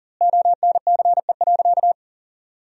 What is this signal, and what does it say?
Morse code, heard as a keyed tone, about 40 wpm, 700 Hz. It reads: ONCE.
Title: 2x Word-Spacing - Sentences from Top 300 Words - 40wpm